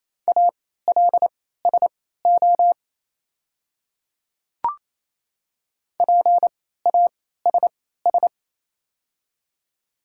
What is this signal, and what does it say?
Morse code, heard as a keyed tone, about 28 wpm, spaced as for 16 wpm, 700 Hz. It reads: ALSO PASS
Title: 3x Character Spacing - Top 500 Words - 28wpm